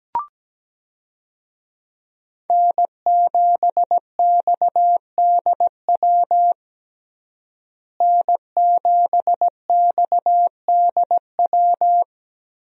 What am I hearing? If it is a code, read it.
Morse code, heard as a keyed tone, about 17 wpm, 700 Hz. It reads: N7XDW N7XDW
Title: USA Call Signs - 5 Characters - Repeated 3x - 17wpm